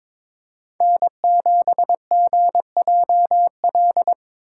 Morse code, heard as a keyed tone, about 23 wpm, 700 Hz. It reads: N7GJL
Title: USA Call Signs - 5 Characters - 8x Character Spacing - 22wpm